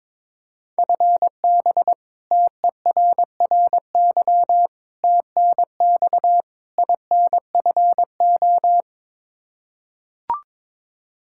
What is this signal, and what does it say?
Morse code, heard as a keyed tone, about 21 wpm, 700 Hz. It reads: FB TERRY TNX INFO
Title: Practice QSO - Exchange 3 - More Info - 22wpm